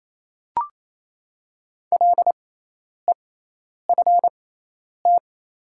Morse code, heard as a keyed tone, about 28 wpm, spaced as for 10 wpm, 700 Hz. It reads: LEFT
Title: 6x Character Spacing - Top 300 Words - 28wpm